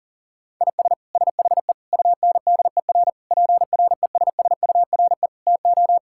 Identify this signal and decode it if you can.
Morse code, heard as a keyed tone, about 37 wpm, 700 Hz. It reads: IS SHE UNDER PRESSURE TO
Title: Sentences from Top 1000 Words - Review Part 6 - 40wpm